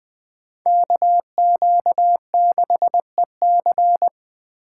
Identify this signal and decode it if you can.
Morse code, heard as a keyed tone, about 20 wpm, 700 Hz. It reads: KQ6EC
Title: Speed Racing - USA 2x2 Call Signs 17wpm